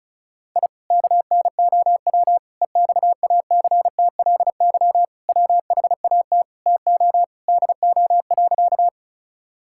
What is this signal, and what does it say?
Morse code, heard as a keyed tone, about 34 wpm, 700 Hz. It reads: I KNOW EXACTLY WHAT TO DO.
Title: Sentences from Top 1000 Words - Review Part 6 - 35wpm